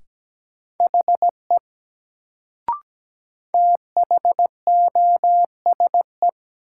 Morse code, heard as a keyed tone, about 17 wpm, 700 Hz. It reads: HE THOSE
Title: Sets of 2 Words - New Word Drill - 17wpm